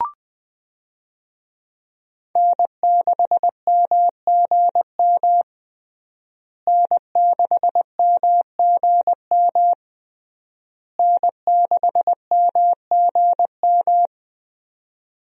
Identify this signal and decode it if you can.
Morse code, heard as a keyed tone, about 20 wpm, 700 Hz. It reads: N6MGM N6MGM N6MGM
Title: USA Call Signs - 5 Characters - Repeated 3x - 20wpm